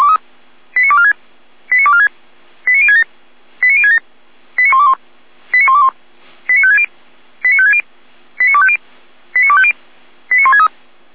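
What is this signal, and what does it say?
Five-tone selective call (ZVEI), heard as quick runs of five tones, tones of about 70 ms. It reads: 89246 89246 89067 89067 89121 89121 89560 89560 89250 89250 89163
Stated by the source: ZVEI five-tone selective call received on SDR.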